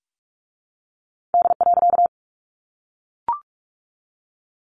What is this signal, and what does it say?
Morse code, heard as a keyed tone, about 45 wpm, 700 Hz. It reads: D.
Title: A Sound Of Thunder 45wpm